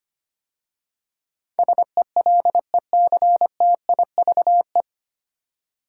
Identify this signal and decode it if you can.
Morse code, heard as a keyed tone, about 25 wpm, 700 Hz. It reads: SELECTIVE
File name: practice / 25wpm / 9 Letter Words - First 3 Letters Given - 25wpm